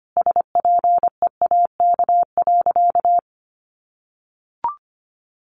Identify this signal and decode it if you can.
Morse code, heard as a keyed tone, about 25 wpm, 700 Hz. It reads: SPEAK.
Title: Sentences from Top 1300 Words 25wpm